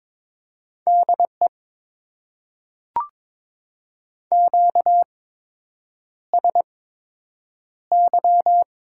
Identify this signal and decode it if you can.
Morse code, heard as a keyed tone, about 22 wpm, spaced as for 8 wpm, 700 Hz. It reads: DE QSY
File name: Top 90 QSO Elements - 8x Character Spacing - 22wpm